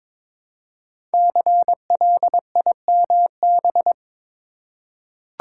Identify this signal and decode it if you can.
Morse code, heard as a keyed tone, about 22 wpm, 700 Hz. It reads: CLIMB
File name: 5 Letter Words - First 3 Letters Given - 22wpm